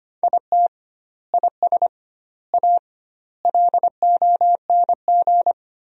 Morse code, heard as a keyed tone, about 25 wpm, 700 Hz. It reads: IT IS A LONG
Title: 2x Word-Spacing - Sentences from Top 300 Words - 25wpm